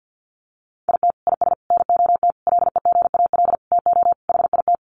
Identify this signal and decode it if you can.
Morse code, heard as a keyed tone, about 50 wpm, 700 Hz. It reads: IT IS NOT LEGAL TO HIT
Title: Sentences from Top 1200 Words 50wpm